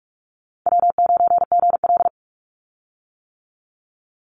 Morse code, heard as a keyed tone, about 44 wpm, 700 Hz. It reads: W9GL
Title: Easy Call Signs - 45wpm